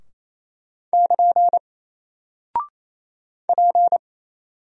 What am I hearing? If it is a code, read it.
Morse code, heard as a keyed tone, about 27 wpm, 700 Hz. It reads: <KN> P
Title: Single Character (Letter:Number:Punctuation:Prosign) 28wpm